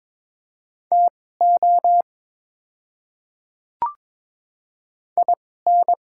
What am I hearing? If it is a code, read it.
Morse code, heard as a keyed tone, about 22 wpm, spaced as for 16 wpm, 700 Hz. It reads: TO IN
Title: 2x Character Spacing - Top 300 Words - 22wpm